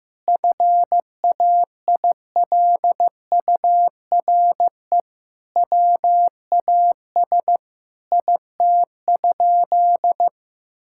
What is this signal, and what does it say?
Morse code, heard as a keyed tone, about 16 wpm, 700 Hz. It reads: FAILURE WAS IT?
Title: Sentences from Top 1000 Words - Review Part 6 - 15wpm